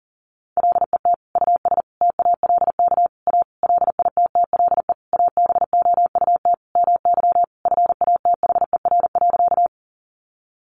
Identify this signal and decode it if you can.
Morse code, heard as a keyed tone, about 40 wpm, 700 Hz. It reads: LET US TALK A LITTLE ABOUT MY FATHER.
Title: Sentences from Top 400 Words 40wpm